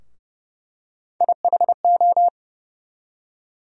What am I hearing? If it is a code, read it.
Morse code, heard as a keyed tone, about 31 wpm, 700 Hz. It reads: IHO
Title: USA Call Sign Postfixes - 3 Letters 30wpm